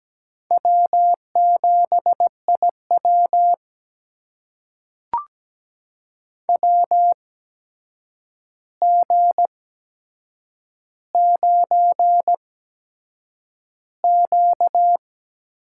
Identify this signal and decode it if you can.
Morse code, heard as a keyed tone, about 17 wpm, spaced as for 7 wpm, 700 Hz. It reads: W7IW WG9Q
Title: USA Call Signs - 4 Characters - 8x Character Spacing - 17wpm